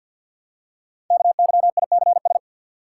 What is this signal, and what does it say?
Morse code, heard as a keyed tone, about 54 wpm, 700 Hz. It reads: KYICS
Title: Code Groups - 5 Letters - 50wpm